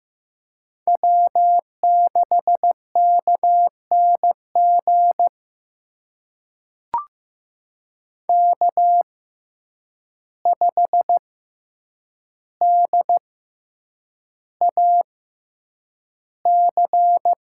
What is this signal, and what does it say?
Morse code, heard as a keyed tone, about 15 wpm, spaced as for 8 wpm, 700 Hz. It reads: W6KNG K5DAC
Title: USA Call Signs - 6x Character Spacing - 15wpm